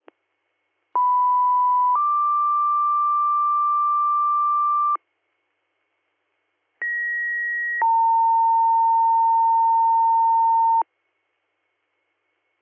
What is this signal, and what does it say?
Two-tone page, heard as a one-second tone, then a three-second tone, about 990 then 1180 Hz, about 1900 then 910 Hz. It is heard from 1.0 to 10.8 s.